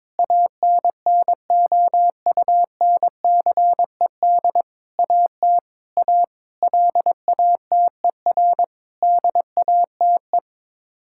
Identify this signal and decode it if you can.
Morse code, heard as a keyed tone, about 22 wpm, 700 Hz. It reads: ANNOUNCED AT A LATER DATE
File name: practice / 22wpm / Sets of 7 Words - 22wpm